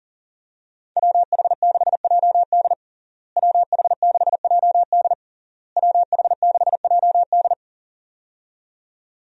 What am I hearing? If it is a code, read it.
Morse code, heard as a keyed tone, about 40 wpm, 700 Hz. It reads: WH6JD WH6JD WH6JD
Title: USA Call Signs - 5 Characters - Repeated 3x - 40wpm